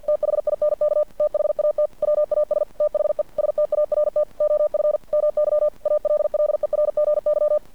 Morse code, heard as a keyed tone, about 50 wpm, 600 Hz. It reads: THINK THAT WAS THE START OF MY ALLERGY
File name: archived / CRSnet